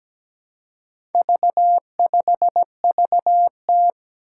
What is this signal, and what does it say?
Morse code, heard as a keyed tone, about 17 wpm, 700 Hz. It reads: V5VT
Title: Four Challenging Letter-Number Combinations 17wpm